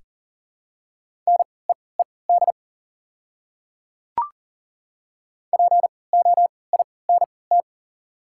Morse code, heard as a keyed tone, about 39 wpm, spaced as for 23 wpm, 700 Hz. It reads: NEED POINT
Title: 3x Character Spacing - Top 200 Words - 40wpm